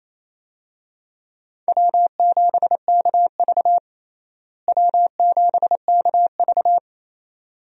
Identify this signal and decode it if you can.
Morse code, heard as a keyed tone, about 28 wpm, 700 Hz. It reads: W7KV W7KV
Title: USA Call Signs - 4 Characters - Repeated 2x - 28wpm